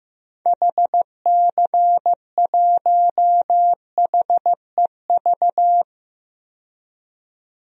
Morse code, heard as a keyed tone, about 15 wpm, 700 Hz. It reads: HC1HEV
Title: DXCC-100 Call Signs - 6 Characters - 15wpm